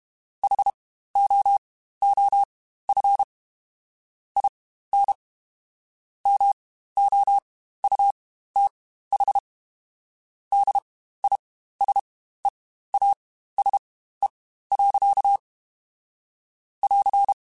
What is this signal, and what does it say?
Morse code, heard as a keyed tone, about 32 wpm, spaced as for 15 wpm, 800 Hz. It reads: HOOF IN MOUTH DISEASE. <AR>
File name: Quotes Spaced / Quotes Spaced-mp3s / HoofInMouth sp @30